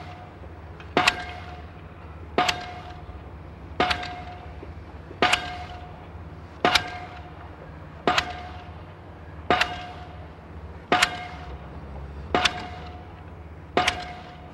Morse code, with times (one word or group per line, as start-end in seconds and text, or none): none